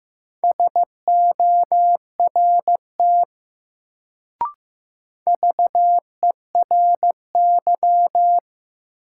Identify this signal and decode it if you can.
Morse code, heard as a keyed tone, about 15 wpm, 700 Hz. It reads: SORT VERY
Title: Sets of 2 Words - New Word Drill - 15wpm